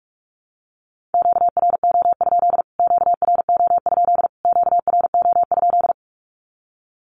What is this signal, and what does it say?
Morse code, heard as a keyed tone, about 44 wpm, 700 Hz. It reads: QRO? QRO? QRO?
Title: Q-Codes Repeated 3x 45wpm